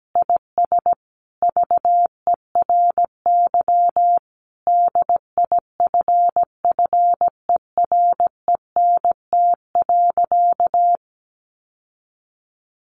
Morse code, heard as a keyed tone, about 17 wpm, 700 Hz. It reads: IS VERY DIFFERENT.
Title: Sentences from Top 700 Words 17wpm